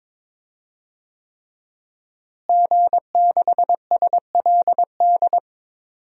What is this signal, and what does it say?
Morse code, heard as a keyed tone, about 22 wpm, 700 Hz. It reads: G6SLD
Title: DXCC-100 Call Signs - 5 Characters - Repeated 3x - 22wpm